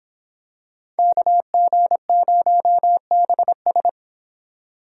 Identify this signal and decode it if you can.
Morse code, heard as a keyed tone, about 27 wpm, 700 Hz. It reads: KG0BS
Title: Speed Racing - USA 2x2 Call Signs 22wpm